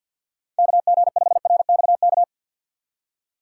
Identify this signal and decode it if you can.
Morse code, heard as a keyed tone, about 48 wpm, 690 Hz. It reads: KG5RXK
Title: USA Call Signs - 3x Character Spacing - 50wpm